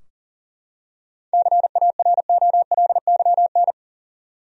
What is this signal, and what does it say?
Morse code, heard as a keyed tone, about 37 wpm, 700 Hz. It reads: CAROLYN
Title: USA Common Names - 40wpm